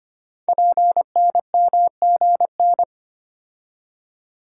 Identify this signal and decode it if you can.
Morse code, heard as a keyed tone, about 25 wpm, 700 Hz. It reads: PNMGN
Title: Code Groups - 5 Letters - 25wpm